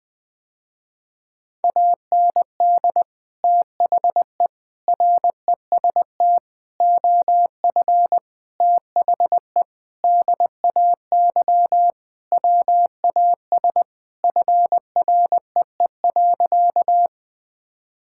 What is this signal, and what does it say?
Morse code, heard as a keyed tone, about 21 wpm, 700 Hz. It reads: AND THE REST OF THE DAY WAS FREE.
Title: Sentences with 4-Word Phrases - Spoken Beginning - 20wpm